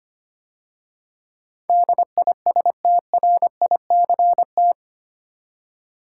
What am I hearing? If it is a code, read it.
Morse code, heard as a keyed tone, about 25 wpm, 700 Hz. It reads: DISTRICT
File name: 8 Letter Words - First 3 Letters Given - 25wpm